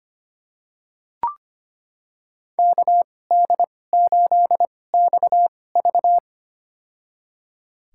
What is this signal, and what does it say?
Morse code, heard as a keyed tone, about 25 wpm, spaced as for 18 wpm, 700 Hz. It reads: KD8XV
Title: USA Call Signs - 5 Characters - 2x Character Spacing - 25wpm